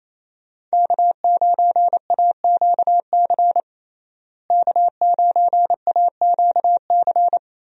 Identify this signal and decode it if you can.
Morse code, heard as a keyed tone, about 28 wpm, 700 Hz. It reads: K9AQC K9AQC
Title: USA Call Signs - 5 Characters - Repeated 2x - 28wpm